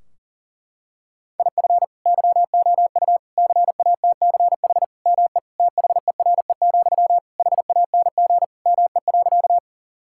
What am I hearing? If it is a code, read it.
Morse code, heard as a keyed tone, about 37 wpm, 700 Hz. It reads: IF YOU CATCH ME THERE, HANG ME.
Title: Sentences from Top 1600 Words 40wpm